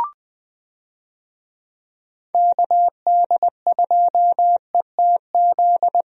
Morse code, heard as a keyed tone, about 20 wpm, 700 Hz. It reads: KD2ETZ
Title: USA Call Signs - 6 Characters - Repeated 3x - 20wpm